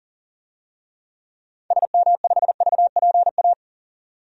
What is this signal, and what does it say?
Morse code, heard as a keyed tone, about 40 wpm, 700 Hz. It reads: SM5VPA
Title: DXCC-100 Call Signs - 6 Characters - 40wpm